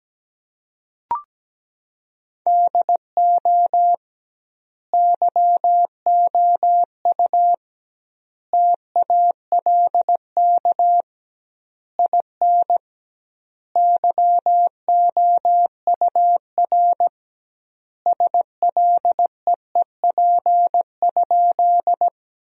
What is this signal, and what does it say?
Morse code, heard as a keyed tone, about 17 wpm, 700 Hz. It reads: DO YOU TALK IN YOUR SLEEP?
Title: 2x Word-Spacing - Sentences from Top 500 Words - 17wpm